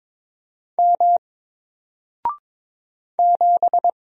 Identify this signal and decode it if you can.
Morse code, heard as a keyed tone, about 22 wpm, 700 Hz. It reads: M 7
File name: Single-Letter-Number-22wpm